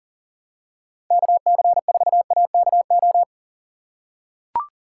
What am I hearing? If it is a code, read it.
Morse code, heard as a keyed tone, about 39 wpm, 700 Hz. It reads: KC4AKO
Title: USA Call Signs - 6 Characters - 3x Character Spacing - 40wpm